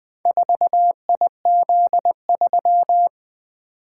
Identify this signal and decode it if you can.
Morse code, heard as a keyed tone, about 20 wpm, 700 Hz. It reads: 4IZ3
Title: Four Letter-Number Combinations - 20wpm